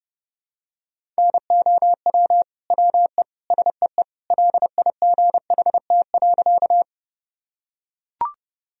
Morse code, heard as a keyed tone, about 30 wpm, 700 Hz. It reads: NOW WE SEE LIGHT.
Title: Sentences from Top 1000 Words - Review Part 1 - 30wpm